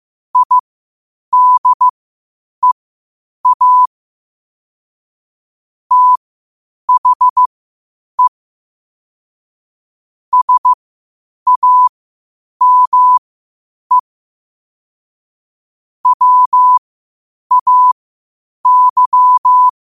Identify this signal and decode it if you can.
Morse code, heard as a keyed tone, about 15 wpm, spaced as for 8 wpm, 1000 Hz. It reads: IDEA THE SAME WAY